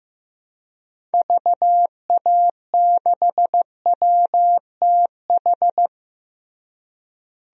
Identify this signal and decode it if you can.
Morse code, heard as a keyed tone, about 15 wpm, 700 Hz. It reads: VA6WTH
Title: DXCC-100 Call Signs - 6 Characters - Repeated 2x - 15wpm